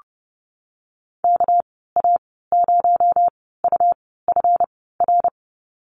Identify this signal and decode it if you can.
Morse code, heard as a keyed tone, about 30 wpm, spaced as for 17 wpm, 700 Hz. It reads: KA0UFR
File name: Lesson 37g - Letter x - Call Signs 30wpm